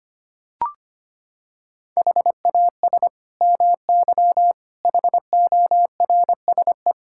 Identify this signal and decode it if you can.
Morse code, heard as a keyed tone, about 25 wpm, 700 Hz. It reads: HAS MY HORSE
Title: Sentences from Top 1600 Words 25wpm